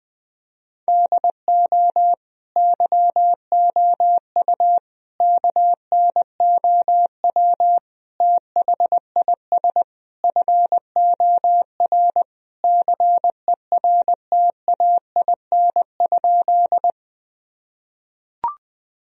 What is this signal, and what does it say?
Morse code, heard as a keyed tone, about 21 wpm, 700 Hz. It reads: DO YOU KNOW THIS FOR CERTAIN?
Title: Sentences from Top 1000 Words - Review Part 2 - 20wpm